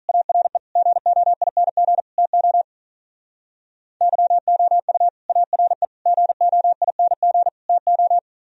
Morse code, heard as a keyed tone, about 42 wpm, 700 Hz. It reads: ARE GOING TO YOU ARE GOING TO
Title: Speed-Racing - Sets of 4 Words 35wpm